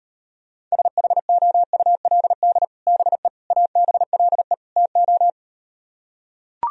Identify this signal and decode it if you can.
Morse code, heard as a keyed tone, about 38 wpm, 700 Hz. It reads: SHOULD BE ABLE TO